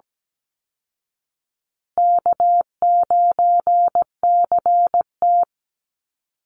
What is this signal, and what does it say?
Morse code, heard as a keyed tone, about 17 wpm, 700 Hz. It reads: K9CT
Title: Easy Call Signs Repeated 3x - 17wpm